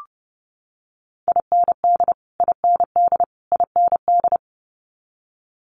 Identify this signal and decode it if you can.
Morse code, heard as a keyed tone, about 30 wpm, 700 Hz. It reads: IND IND IND
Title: Multi-letter Phonemes Repeated 3x 30wpm